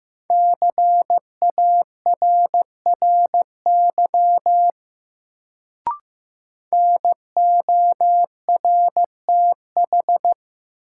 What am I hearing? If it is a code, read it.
Morse code, heard as a keyed tone, about 15 wpm, 700 Hz. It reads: CARRY NORTH
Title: Sentences from Top 400 Words - New Word Drill - 15wpm